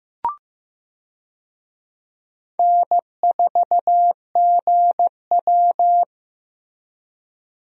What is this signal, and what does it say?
Morse code, heard as a keyed tone, about 15 wpm, 700 Hz. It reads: N4GW
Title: USA Call Signs - 4 Characters - Repeated 2x - 15wpm